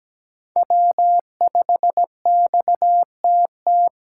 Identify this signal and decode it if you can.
Morse code, heard as a keyed tone, about 17 wpm, 700 Hz. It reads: W5XTT
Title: USA Call Signs - 5 Characters - 8x Character Spacing - 17wpm